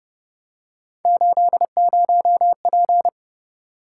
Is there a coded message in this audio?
Yes: Morse code from 1.1 to 3.1 s, heard as a keyed tone, about 31 wpm, 700 Hz. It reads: 80P